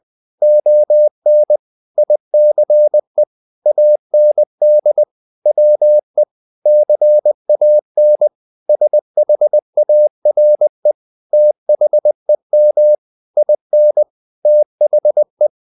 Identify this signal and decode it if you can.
Morse code, heard as a keyed tone, about 20 wpm, 600 Hz. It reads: ON ICE AND WE CAN SHARE THEM IN THE
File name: the old man and the sea / the old man and the sea part 1 20wpm